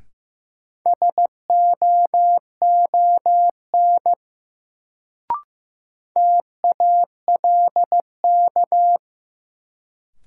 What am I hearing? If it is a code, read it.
Morse code, heard as a keyed tone, about 16 wpm, 700 Hz. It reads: SOON TALK
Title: Sentences from Top 400 Words - New Word Drill - 15wpm